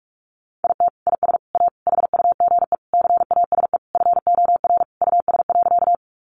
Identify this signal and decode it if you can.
Morse code, heard as a keyed tone, about 44 wpm, 700 Hz. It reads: IT IS A HUGE CASE FOR US.
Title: Sentences from Top 1100 Words 45wpm